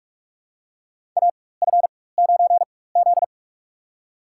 Morse code, heard as a keyed tone, about 46 wpm, spaced as for 21 wpm, 700 Hz. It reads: AF9Z
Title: USA Call Signs - 4 Characters - 4x Character Spacing - 45wpm